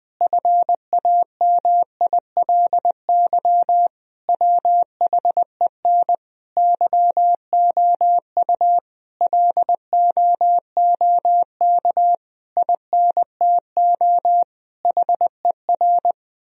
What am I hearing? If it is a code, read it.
Morse code, heard as a keyed tone, about 20 wpm, 700 Hz. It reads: FAMILY WHEN YOU LOOK INTO HER
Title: Sentences from Top 1000 Words - Review Part 4 - 20wpm